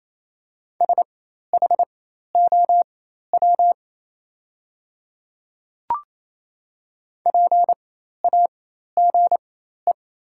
Morse code, heard as a keyed tone, about 28 wpm, spaced as for 13 wpm, 700 Hz. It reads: SHOW PAGE
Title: 4x Character Spacing - Top 500 Words - 28wpm